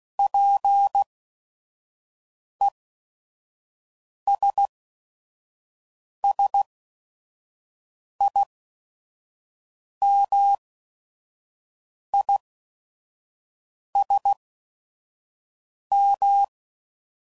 Morse code, heard as a keyed tone, about 16 wpm, spaced as for 5 wpm, 800 Hz. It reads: PESSIMISM